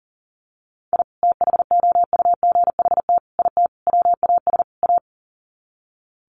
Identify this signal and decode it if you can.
Morse code, heard as a keyed tone, about 40 wpm, 700 Hz. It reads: I THOUGHT IT WAS A
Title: Sets of 5 Words 40wpm